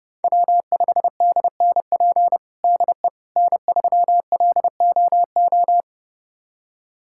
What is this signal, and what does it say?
Morse code, heard as a keyed tone, about 32 wpm, 700 Hz. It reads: W5DNP DE N3LOO
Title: USA Callsign DE Callsign - 5 Characters - 30wpm